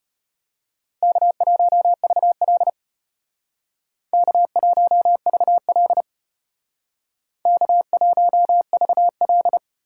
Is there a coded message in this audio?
Yes: Morse code from 1.0 to 9.6 s, heard as a keyed tone, about 34 wpm, 700 Hz. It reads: K1VL K1VL K1VL